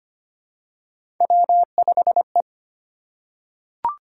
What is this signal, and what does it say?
Morse code, heard as a keyed tone, about 25 wpm, 700 Hz. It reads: W5E
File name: Three Letter-Number Combinations - 25wpm